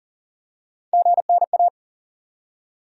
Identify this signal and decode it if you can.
Morse code, heard as a keyed tone, about 40 wpm, 700 Hz. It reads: GNA
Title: USA Call Sign Postfixes - 3 Letters 40wpm